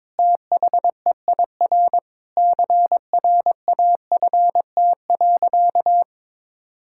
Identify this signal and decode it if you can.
Morse code, heard as a keyed tone, about 22 wpm, 700 Hz. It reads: THEIR CRAFT.
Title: Sentences from Top 1000 Words - Review Part 6 - 22wpm